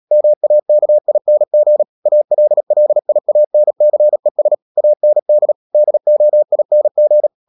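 Morse code, heard as a keyed tone, about 37 wpm, 600 Hz. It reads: MAKING ALLIANCES AND DOING
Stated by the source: Morse code practice at 35–39 wpm sending English text of Ken Follett's novel Never, from CW med Gnister.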